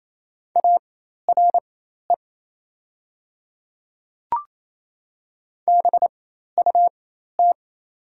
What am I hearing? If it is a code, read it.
Morse code, heard as a keyed tone, about 28 wpm, spaced as for 13 wpm, 700 Hz. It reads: ARE BUT
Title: 4x Character Spacing - Top 100 Words - 28wpm